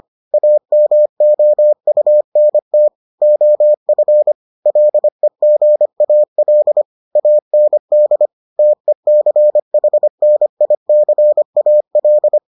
Morse code, heard as a keyed tone, about 25 wpm, 600 Hz. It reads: AMOUNT OF LEGAL AND TECHNICAL